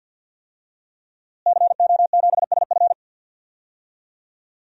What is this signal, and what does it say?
Morse code, heard as a keyed tone, about 49 wpm, 700 Hz. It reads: CO7SF
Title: DXCC-100 Call Signs - 5 Characters - Repeated 2x - 50wpm